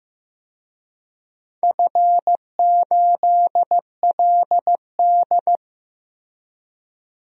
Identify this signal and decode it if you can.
Morse code, heard as a keyed tone, about 15 wpm, 700 Hz. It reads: F8LD
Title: DXCC-100 Call Signs - 4 Characters - Repeated 2x - 15wpm